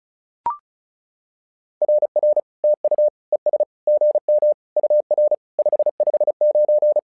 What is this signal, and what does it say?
Morse code, heard as a keyed tone, about 36 wpm, 600 Hz. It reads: RR TU ES GM UR 559
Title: SOTA QSO Practice 35wpm